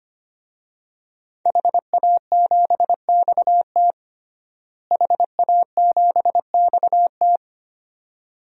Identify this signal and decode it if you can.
Morse code, heard as a keyed tone, about 25 wpm, 700 Hz. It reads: HA7XT HA7XT
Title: DXCC-100 Call Signs - 5 Characters - Repeated 2x - 25wpm